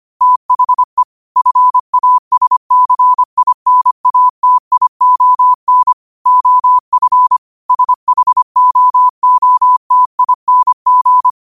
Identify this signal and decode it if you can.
Morse code, heard as a keyed tone, about 25 wpm, 1000 Hz. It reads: THE FASCINATION OF SHOOTING
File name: 25wpm